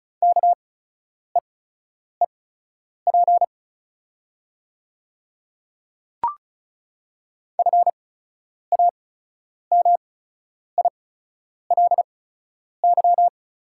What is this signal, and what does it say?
Morse code, heard as a keyed tone, about 36 wpm, spaced as for 10 wpm, 690 Hz. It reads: KEEP FAMILY
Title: 8x Character Spacing - Top 500 Words - 35wpm